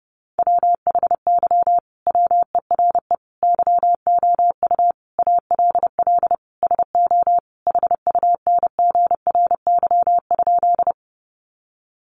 Morse code, heard as a keyed tone, about 28 wpm, 700 Hz. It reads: WHY WERE YOU ALL SO HUNGRY?
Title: Sentences from Top 1400 Words 30wpm